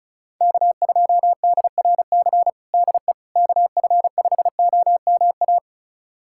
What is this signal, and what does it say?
Morse code, heard as a keyed tone, about 36 wpm, 700 Hz. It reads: K2DRC DE KF5OMA
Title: USA Callsign DE Callsign 35wpm